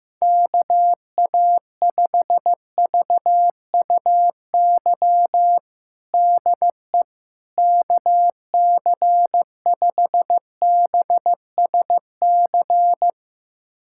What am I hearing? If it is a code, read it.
Morse code, heard as a keyed tone, about 15 wpm, 700 Hz. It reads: KA5VUY DE KC5BSC
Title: USA Callsign DE Callsign - 6 Characters - 15wpm